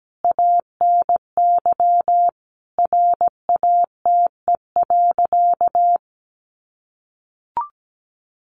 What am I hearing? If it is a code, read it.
Morse code, heard as a keyed tone, about 17 wpm, 700 Hz. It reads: ANY RATE.
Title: Sentences from Top 900 Words 17wpm